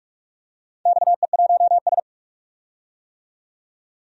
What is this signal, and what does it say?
Morse code, heard as a keyed tone, about 44 wpm, 700 Hz. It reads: XE1S
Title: DXCC-100 Call Signs - 4 Characters - 45wpm